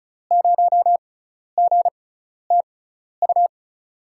Morse code, heard as a keyed tone, about 34 wpm, spaced as for 12 wpm, 700 Hz. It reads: 0GTU
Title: USA Call Signs - 6 Characters - 6x Character Spacing - 35wpm